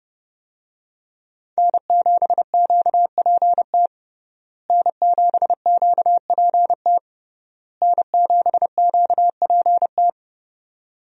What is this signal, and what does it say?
Morse code, heard as a keyed tone, about 30 wpm, 700 Hz. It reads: N7QPT N7QPT N7QPT
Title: USA Call Signs - 5 Characters - Repeated 3x - 30wpm